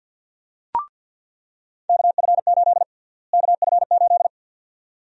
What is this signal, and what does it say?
Morse code, heard as a keyed tone, about 49 wpm, 700 Hz. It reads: KF8 KF8
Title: USA Call Sign Prefixes - Repeated 2x - 50wpm